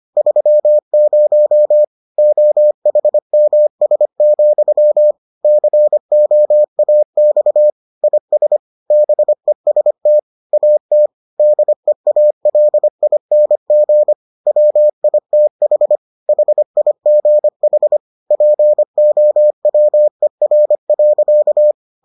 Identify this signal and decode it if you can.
Morse code, heard as a keyed tone, about 25 wpm, 600 Hz. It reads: 30 OHMS, COAX IS BEST AT DEALING WITH HIGH POWER.